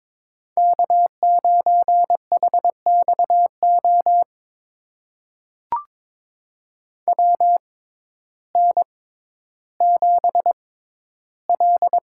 Morse code, heard as a keyed tone, about 22 wpm, spaced as for 12 wpm, 700 Hz. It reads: K9HXO WN7L
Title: USA Call Signs - 6x Character Spacing - 22wpm